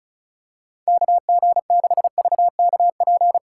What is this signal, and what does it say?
Morse code, heard as a keyed tone, about 36 wpm, 700 Hz. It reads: KG6VKP
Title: USA Call Signs - 6 Characters - 2x Character Spacing - 35wpm